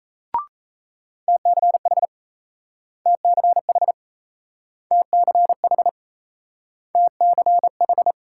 Morse code, heard as a keyed tone, about 38 wpm, 700 Hz. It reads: TCH TCH TCH TCH